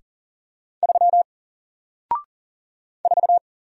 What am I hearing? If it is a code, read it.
Morse code, heard as a keyed tone, about 40 wpm, 700 Hz. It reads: 3 4
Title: Single Letter-Number Alt 40wpm